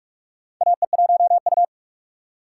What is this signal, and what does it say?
Morse code, heard as a keyed tone, about 44 wpm, 700 Hz. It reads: AE1U
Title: USA Call Signs - 4x Character Spacing - 45wpm